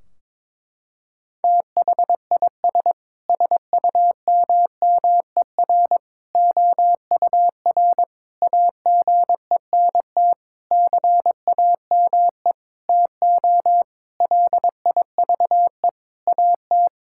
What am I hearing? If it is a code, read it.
Morse code, heard as a keyed tone, about 22 wpm, 700 Hz. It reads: THIS SUMMER OUR AGENT CAME TO LIVE AT
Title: Sentences from Top 1600 Words 22wpm